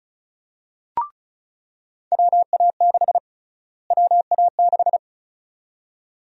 Morse code, heard as a keyed tone, about 36 wpm, 700 Hz. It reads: WA6 WA6
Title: USA Call Sign Prefixes - Repeated 2x - 35wpm